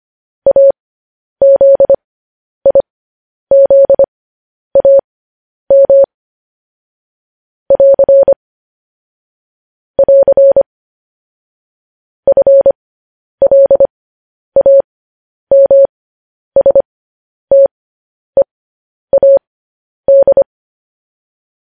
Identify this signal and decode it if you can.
Morse code, heard as a keyed tone, about 25 wpm, spaced as for 10 wpm, 550 Hz. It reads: AZIZAM <AR> <AR> FLAMSTEAD
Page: news headlines in morse code at 10 wpm